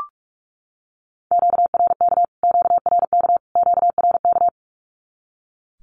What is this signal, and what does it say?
Morse code, heard as a keyed tone, about 44 wpm, 700 Hz. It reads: QRK QRK QRK
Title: Q-Codes Repeated 3x 45wpm